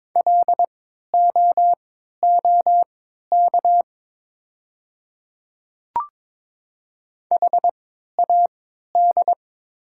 Morse code, heard as a keyed tone, about 22 wpm, spaced as for 12 wpm, 690 Hz. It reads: LOOK HAD
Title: 3x Character Spacing - Top 100 Words - 22wpm